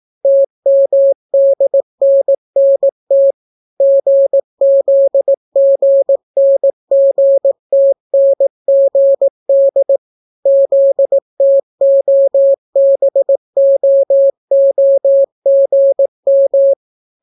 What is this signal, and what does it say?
Morse code, heard as a keyed tone, about 18 wpm, 550 Hz. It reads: TMDNNT GZGNGTNGD ZTOBOOGM